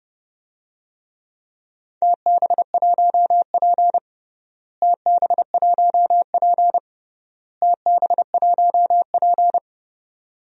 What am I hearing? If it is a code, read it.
Morse code, heard as a keyed tone, about 30 wpm, 700 Hz. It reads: TB1P TB1P TB1P
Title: DXCC-100 Call Signs - 4 Characters - Repeated 3x - 30wpm